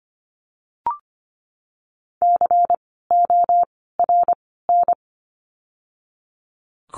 Morse code, heard as a keyed tone, about 25 wpm, 700 Hz. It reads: C O R N
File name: course / 25:10wpm / 094 Lesson 16f - Letter c - Words 25wpm